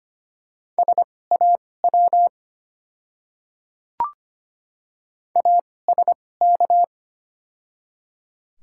Morse code, heard as a keyed tone, about 25 wpm, spaced as for 18 wpm, 700 Hz. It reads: SAW ASK
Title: 2x Character Spacing - Top 500 Words - 25wpm